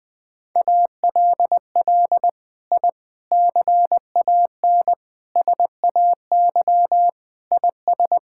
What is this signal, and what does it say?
Morse code, heard as a keyed tone, about 20 wpm, 700 Hz. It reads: ALL I CAN SAY IS
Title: Sets of 6 Words - 20wpm